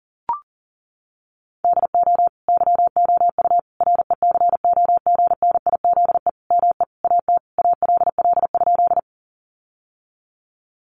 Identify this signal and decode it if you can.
Morse code, heard as a keyed tone, about 40 wpm, 700 Hz. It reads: DO YOU RECOGNIZE ME AT ALL?
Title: Sentences from Top 1400 Words 40wpm